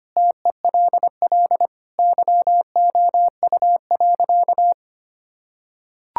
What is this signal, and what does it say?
Morse code, heard as a keyed tone, about 25 wpm, 700 Hz. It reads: TELL YOU.
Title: Sentences from Top 1000 Words - Review Part 6 - 25wpm